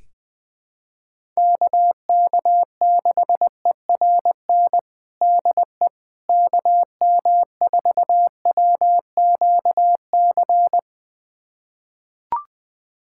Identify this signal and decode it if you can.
Morse code, heard as a keyed tone, about 20 wpm, 700 Hz. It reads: KK6ERN DE KM4WQC
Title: USA Callsign DE Callsign - 6 Characters - 20wpm